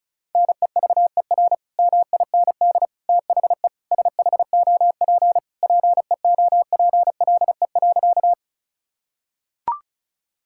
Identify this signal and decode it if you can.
Morse code, heard as a keyed tone, about 36 wpm, 700 Hz. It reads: NEVER MIND THE SHOP PEOPLE.
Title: Sentences from Top 1600 Words 35wpm